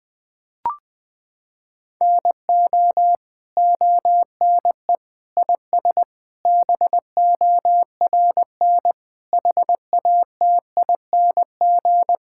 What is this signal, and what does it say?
Morse code, heard as a keyed tone, about 20 wpm, 700 Hz. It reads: NO ONE IS BORN HATING
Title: Sentences with 9-Word Phrases - 20wpm